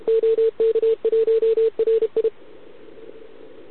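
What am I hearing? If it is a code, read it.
Morse code, heard as a keyed tone, about 31 wpm, 440 Hz. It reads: OK1RI
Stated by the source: Contest CW from OK1RI on 80 m during the OKOM DX contest, recorded by a QRP station using a KX3 and a wire antenna.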